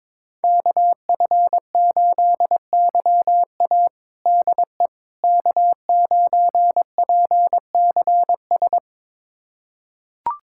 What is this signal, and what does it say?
Morse code, heard as a keyed tone, about 22 wpm, 700 Hz. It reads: KF8YA DE K9PCS
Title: USA Callsign DE Callsign - 5 Characters - 22wpm